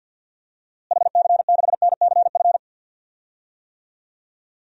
Morse code, heard as a keyed tone, about 51 wpm, 700 Hz. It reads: HC6NCF
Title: DXCC-100 Call Signs - 6 Characters - Repeated 3x - 50wpm